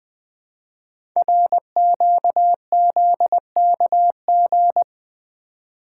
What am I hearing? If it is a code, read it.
Morse code, heard as a keyed tone, about 20 wpm, 700 Hz. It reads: RQZKG
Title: Code Groups - 5 Letter:Numbers - 20wpm